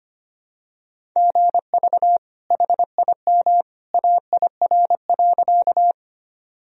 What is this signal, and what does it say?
Morse code, heard as a keyed tone, about 25 wpm, 700 Hz. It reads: GV HIM AIR.
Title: Sentences from Top 200 Words and Abbreviations 25wpm